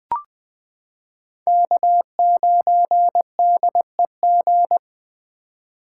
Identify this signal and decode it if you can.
Morse code, heard as a keyed tone, about 20 wpm, 700 Hz. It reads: K9DEG
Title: 2023 POTA Hunter Call Signs - 20wpm